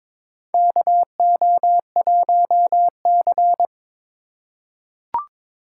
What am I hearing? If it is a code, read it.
Morse code, heard as a keyed tone, about 22 wpm, 700 Hz. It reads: KO1C
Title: USA Call Signs - 4 Characters - 4x Character Spacing - 22wpm